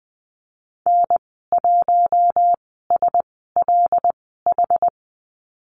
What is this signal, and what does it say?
Morse code, heard as a keyed tone, about 20 wpm, spaced as for 15 wpm, 700 Hz. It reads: N1SLH
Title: 076 Lesson 13g - Letter L - Call Signs 20wpm